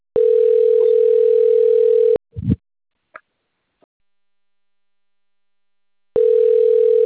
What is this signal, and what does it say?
Ringback tone (North America), 440 + 480 Hz.